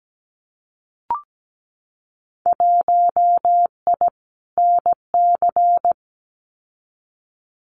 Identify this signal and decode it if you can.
Morse code, heard as a keyed tone, about 17 wpm, 700 Hz. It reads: 1I NC
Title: Winter Field Day Exchanges 17wpm